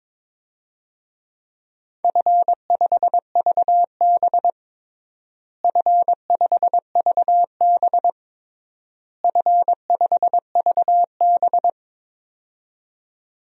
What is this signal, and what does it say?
Morse code, heard as a keyed tone, about 22 wpm, 700 Hz. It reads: F5VB F5VB F5VB
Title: DXCC-100 Call Signs - 4 Characters - Repeated 3x - 22wpm